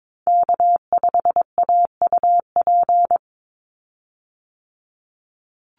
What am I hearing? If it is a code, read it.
Morse code, heard as a keyed tone, about 22 wpm, 700 Hz. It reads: K5AUP
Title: Call Signs Repeated 2x - 22wpm